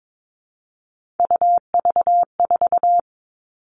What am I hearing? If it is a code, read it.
Morse code, heard as a keyed tone, about 22 wpm, 700 Hz. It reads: UV4
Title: All Cut Numbers - 3-Digit 22wpm